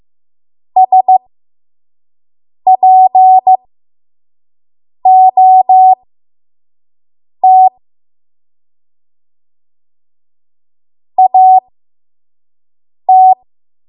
Morse code, heard as a keyed tone, about 15 wpm, spaced as for 5 wpm, 750 Hz. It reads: SPOT AT
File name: Archive / 5 WPM / 05WPM